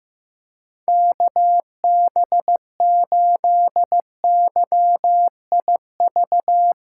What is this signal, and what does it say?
Morse code, heard as a keyed tone, about 15 wpm, 700 Hz. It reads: KB8YIV
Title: USA Callsign DE Callsign - 6 Characters - 15wpm